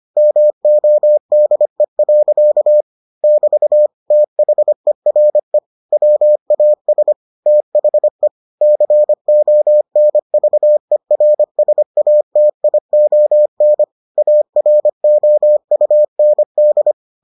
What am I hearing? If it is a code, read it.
Morse code, heard as a keyed tone, about 25 wpm, 600 Hz. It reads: MODE. <BT> THERE WAS THE CONVERSATION AROUND